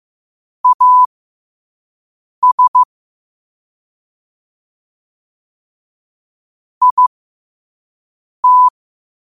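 Morse code, heard as a keyed tone, about 15 wpm, spaced as for 5 wpm, 1000 Hz. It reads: AS IT
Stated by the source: Morse code sent at 5 wpm.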